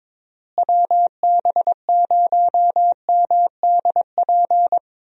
Morse code, heard as a keyed tone, about 22 wpm, 700 Hz. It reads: WB0MDP
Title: USA Call Signs - 6 Characters - Repeated 3x - 22wpm